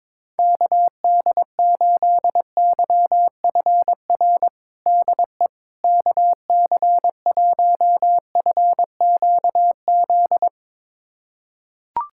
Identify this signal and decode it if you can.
Morse code, heard as a keyed tone, about 22 wpm, 700 Hz. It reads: KD8YFR DE KC1FQZ
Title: USA Callsign DE Callsign 22wpm